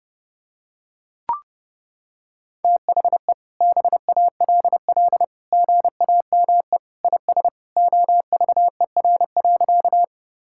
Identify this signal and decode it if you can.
Morse code, heard as a keyed tone, about 30 wpm, 700 Hz. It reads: THE BALL GAME IS OVER.
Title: Sentences from Top 1000 Words - Review Part 6 - 30wpm